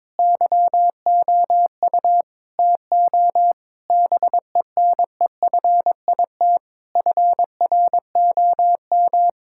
Morse code, heard as a keyed tone, about 22 wpm, 700 Hz. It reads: YOU TO BENEFIT FROM
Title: Sentences from Top 1000 Words - Review Part 6 - 22wpm